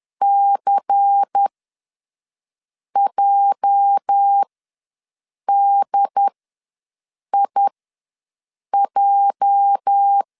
Morse code, heard as a keyed tone, about 10 wpm, spaced as for 5 wpm, 800 Hz. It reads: CJDIJ
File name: LEZIONI / 8 YJ45 / 8 YJ ws 10wpm 4043